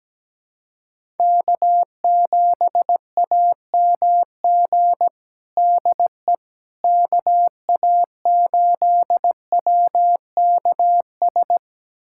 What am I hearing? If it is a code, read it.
Morse code, heard as a keyed tone, about 17 wpm, 700 Hz. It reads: K7AMG DE KA8WKS
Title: USA Callsign DE Callsign 17wpm